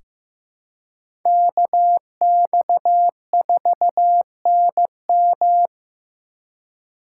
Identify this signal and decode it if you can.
Morse code, heard as a keyed tone, about 15 wpm, 700 Hz. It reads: KX4NM
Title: K1USN SST QSO Practice - 15wpm